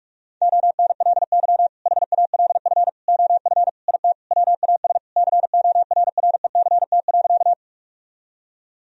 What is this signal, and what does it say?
Morse code, heard as a keyed tone, about 41 wpm, 700 Hz. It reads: ONLY HALF OF IT WAS CORRECT.
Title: Sentences from Top 1000 Words - Review Part 2 - 45wpm